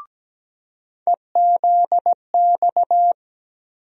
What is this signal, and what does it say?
Morse code, heard as a keyed tone, about 17 wpm, 700 Hz. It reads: EZX
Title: USA Call Sign Postfixes - 3 Letters 17wpm